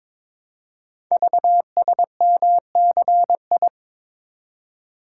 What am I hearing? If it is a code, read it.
Morse code, heard as a keyed tone, about 22 wpm, 700 Hz. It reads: VSMCI
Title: Code Groups - 5 Letter:Numbers - 22wpm